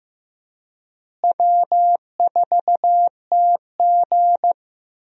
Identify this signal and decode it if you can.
Morse code, heard as a keyed tone, about 16 wpm, 700 Hz. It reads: W4TG
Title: USA Call Signs - 4 Characters - Repeated 3x - 15wpm